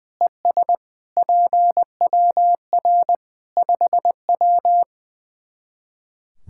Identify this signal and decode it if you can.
Morse code, heard as a keyed tone, about 21 wpm, 700 Hz. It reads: ES PWR 5W
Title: Practice QSO - Exchange 2 - Info - 20wpm